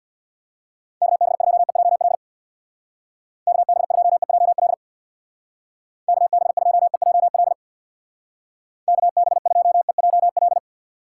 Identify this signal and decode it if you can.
Morse code, heard as a keyed tone, about 59 wpm, 700 Hz. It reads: KB2EJL KB2EJL KB2EJL KB2EJL
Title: Speed Racing - USA 2x3 Call Signs 50wpm